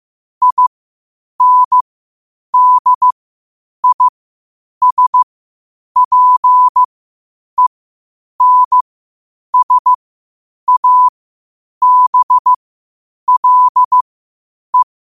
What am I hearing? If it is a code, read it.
Morse code, heard as a keyed tone, about 15 wpm, spaced as for 8 wpm, 1000 Hz. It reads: INDISPENSABLE
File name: qotdfiles / 8wpm